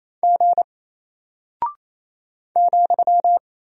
Morse code, heard as a keyed tone, about 28 wpm, 700 Hz. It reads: G ,
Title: Single Character (Letter:Number:Punctuation:Prosign) Alt 28wpm